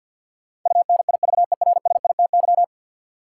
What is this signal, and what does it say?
Morse code, heard as a keyed tone, about 52 wpm, 700 Hz. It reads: UNIVERSITY
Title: Sets of 4 Words - New Word Drill - 50wpm